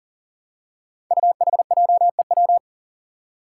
Morse code, heard as a keyed tone, about 40 wpm, 700 Hz. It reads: UHJEW
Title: Five Challenging Letter-Number Combinations 40wpm